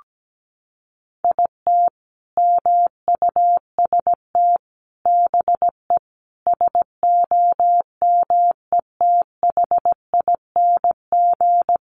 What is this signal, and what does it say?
Morse code, heard as a keyed tone, about 17 wpm, 700 Hz. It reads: IT MUST BE SOMETHING
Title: Sentences from Top 600 Words 17wpm